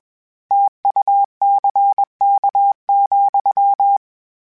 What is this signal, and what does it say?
Morse code, heard as a keyed tone, about 21 wpm, 800 Hz. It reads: TUCK,
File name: Misc / Misc-mp3s / Word Pairs @20